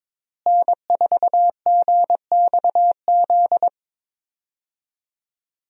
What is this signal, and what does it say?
Morse code, heard as a keyed tone, about 22 wpm, 700 Hz. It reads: N4GXZ
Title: USA Call Signs - 5 Characters - Repeated 3x - 22wpm